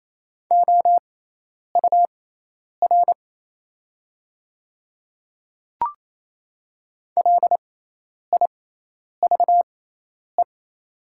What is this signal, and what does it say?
Morse code, heard as a keyed tone, about 27 wpm, spaced as for 10 wpm, 700 Hz. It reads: OUR LIVE
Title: 6x Character Spacing - Top 200 Words - 28wpm